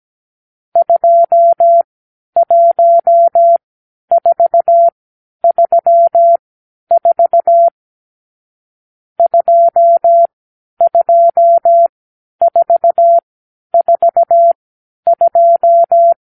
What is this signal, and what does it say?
Morse code, heard as a keyed tone, about 17 wpm, spaced as for 10 wpm, 680 Hz. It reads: 21434 22442
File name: arrl / Review (1-5)